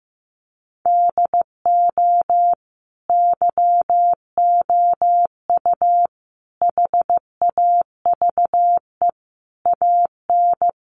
Morse code, heard as a keyed tone, about 15 wpm, 700 Hz. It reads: DO YOU HAVE AN
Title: Sentences from Top 1300 Words 15wpm